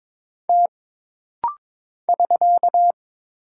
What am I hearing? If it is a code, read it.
Morse code, heard as a keyed tone, about 22 wpm, 700 Hz. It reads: T <SK>
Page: This rapid-fire timing is geared to push your ICR (Instant Character Recognition) skills.